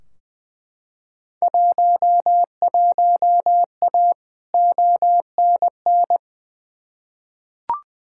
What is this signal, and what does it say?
Morse code, heard as a keyed tone, about 20 wpm, 700 Hz. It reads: 11A ONN
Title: Field Day Exchanges Encore - 20wpm